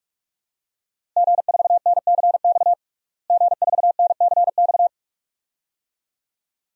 Morse code, heard as a keyed tone, about 42 wpm, 700 Hz. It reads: G4NCX G4NCX
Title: DXCC-100 Call Signs - 5 Characters - Repeated 2x - 45wpm